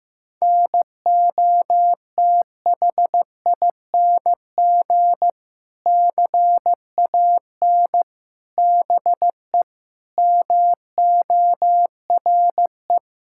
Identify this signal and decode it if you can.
Morse code, heard as a keyed tone, about 15 wpm, 700 Hz. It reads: NOTHING CAN BE MORE
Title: Sentences from Top 1600 Words 15wpm